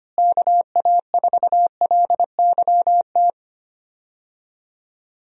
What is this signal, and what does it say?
Morse code, heard as a keyed tone, about 25 wpm, 700 Hz. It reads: KA4LYT
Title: USA Call Signs - 6 Characters - Repeated 3x - 25wpm